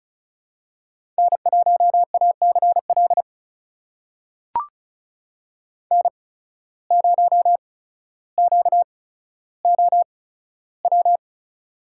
Morse code, heard as a keyed tone, about 34 wpm, spaced as for 15 wpm, 700 Hz. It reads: N1ACL N0QOW
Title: USA Call Signs - 5 Characters - 8x Character Spacing - 35wpm